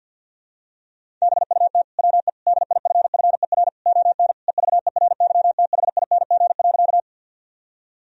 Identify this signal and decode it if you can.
Morse code, heard as a keyed tone, about 51 wpm, 700 Hz. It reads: BUT WE DIFFER ON EVERYTHING.
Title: Sentences from Top 1000 Words - Review Part 1 - 50wpm